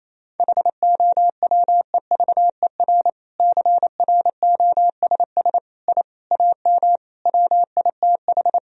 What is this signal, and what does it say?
Morse code, heard as a keyed tone, about 28 wpm, 700 Hz. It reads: HOWEVER CROSS I AM WITH
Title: Sentences from Top 1000 Words - Review Part 1 - 28wpm